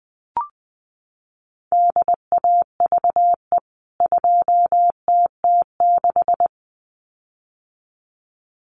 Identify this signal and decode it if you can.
Morse code, heard as a keyed tone, about 20 wpm, 700 Hz. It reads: DAVE 2TT6
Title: CWT Exchange 20wpm